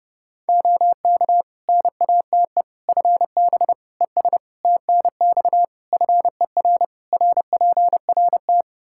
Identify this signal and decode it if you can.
Morse code, heard as a keyed tone, about 32 wpm, 700 Hz. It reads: OK NATE FB ES TNX FER RPRT
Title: Practice QSO - Exchange 2 - Info - 30wpm